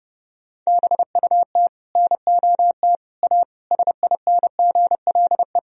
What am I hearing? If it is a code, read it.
Morse code, heard as a keyed tone, about 32 wpm, 700 Hz. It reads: BUT NOT A SINGLE
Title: Sentences from Top 1000 Words - Review Part 4 - 30wpm